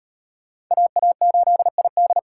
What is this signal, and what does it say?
Morse code, heard as a keyed tone, about 34 wpm, 700 Hz. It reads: AA8ID